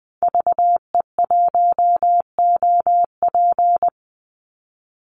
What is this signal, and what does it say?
Morse code, heard as a keyed tone, about 20 wpm, 700 Hz. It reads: VE1OP
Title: Call Signs Repeated 2x - 20wpm